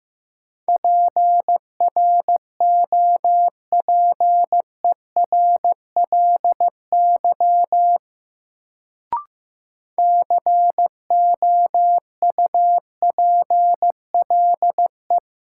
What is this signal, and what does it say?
Morse code, heard as a keyed tone, about 15 wpm, 700 Hz. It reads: PROPERLY COUPLE
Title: Sentences from Top 1100 Words - New Word Drill - 15wpm